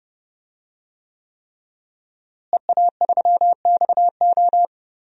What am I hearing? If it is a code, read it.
Morse code, heard as a keyed tone, about 30 wpm, 700 Hz. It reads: EA3XO
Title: DXCC-100 Call Signs - 5 Characters - Repeated 3x - 30wpm